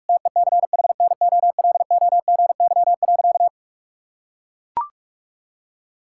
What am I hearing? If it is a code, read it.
Morse code, heard as a keyed tone, about 41 wpm, 700 Hz. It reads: TECHNOLOGY.
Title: Sentences from Top 1000 Words - Review Part 5 - 45wpm